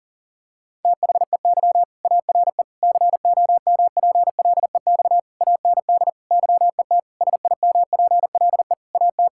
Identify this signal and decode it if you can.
Morse code, heard as a keyed tone, about 40 wpm, 700 Hz. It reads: THEY ARE COMPLEX AND YET SIMPLE AT
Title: Sentences from Top 1000 Words - Review Part 6 - 40wpm